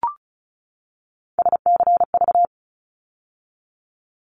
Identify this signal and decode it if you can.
Morse code, heard as a keyed tone, about 34 wpm, 700 Hz. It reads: SCV
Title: ARRL Sections 35wpm